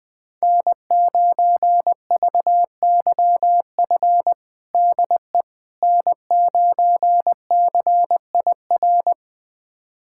Morse code, heard as a keyed tone, about 20 wpm, 700 Hz. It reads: N9VYF DE N9CIR
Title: USA Callsign DE Callsign - 5 Characters - 20wpm